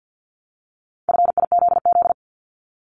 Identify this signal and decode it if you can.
Morse code, heard as a keyed tone, about 50 wpm, 700 Hz. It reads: FIZZ
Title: Multi-letter Phonemes Repeated 3x 50wpm